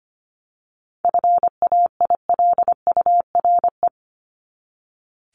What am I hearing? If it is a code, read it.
Morse code, heard as a keyed tone, about 25 wpm, 700 Hz. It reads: FAILURE
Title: Top 751-1000 Words 25wpm